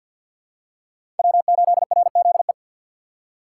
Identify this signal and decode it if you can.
Morse code, heard as a keyed tone, about 52 wpm, 700 Hz. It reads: W8RZE
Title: USA Call Signs - 2x Character Spacing - 50wpm